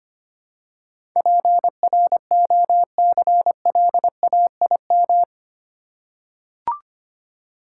Morse code, heard as a keyed tone, about 25 wpm, 700 Hz. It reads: PROCLAIM
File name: Eight Letter Words - 25wpm